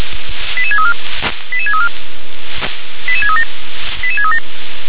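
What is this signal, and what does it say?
Five-tone selective call (ZVEI), heard as quick runs of five tones, tones of about 70 ms. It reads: 88634 88634 88637 88637
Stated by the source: ZVEI five-tone call received on SDR.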